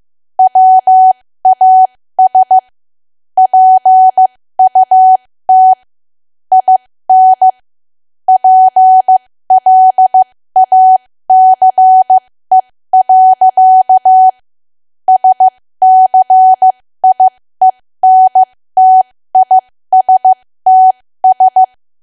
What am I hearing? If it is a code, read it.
Morse code, heard as a keyed tone, about 15 wpm, spaced as for 13 wpm, 750 Hz. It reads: WAS PUT IN PLACE. SCIENTISTS
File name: cwa / W1AW-Level2 / 13WPM